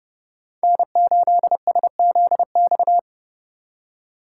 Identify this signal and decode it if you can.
Morse code, heard as a keyed tone, about 35 wpm, 700 Hz. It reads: N8SZX
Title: Speed Racing - USA 1x3 Call Signs 30wpm